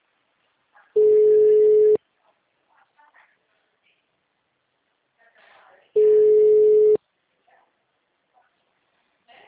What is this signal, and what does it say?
Ringback tone (Europe, 425 Hz), with speech.
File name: out- -2036- - - .23173